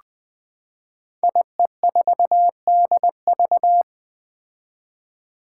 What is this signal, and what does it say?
Morse code, heard as a keyed tone, about 20 wpm, 700 Hz. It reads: IE4DV
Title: Five Challenging Letter-Number Combinations 20wpm